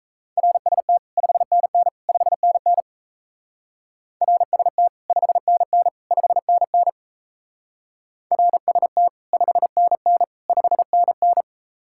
Morse code, heard as a keyed tone, about 35 wpm, 700 Hz. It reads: RST 5NN 5NN RST 5NN 5NN RST 5NN 5NN
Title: Speed-Racing Common QSO Phrases - 28wpm